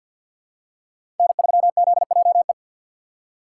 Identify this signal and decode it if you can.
Morse code, heard as a keyed tone, about 52 wpm, 700 Hz. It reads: N3ZJE
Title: USA Call Signs - 5 Characters - 2x Character Spacing - 50wpm